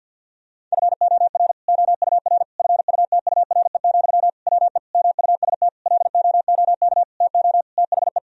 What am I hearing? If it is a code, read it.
Morse code, heard as a keyed tone, about 47 wpm, 700 Hz. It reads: FOR OUR FUTURE, WE MUST LOOK TO THE
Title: Sentences from Top 1000 Words - Review Part 3 - 50wpm